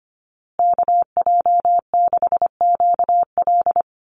Morse code, heard as a keyed tone, about 25 wpm, 700 Hz. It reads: KJ6QL
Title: USA Call Signs Repeated 3x 25wpm